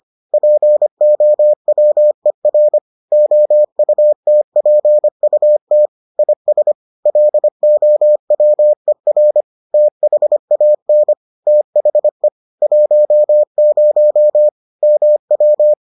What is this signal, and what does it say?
Morse code, heard as a keyed tone, about 25 wpm, 600 Hz. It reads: POWER OUTPUT IS LOWER THAN THE 10 MW